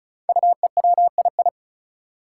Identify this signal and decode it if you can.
Morse code, heard as a keyed tone, about 36 wpm, 700 Hz. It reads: UEWII
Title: Five Challenging Letter-Number Combinations 35wpm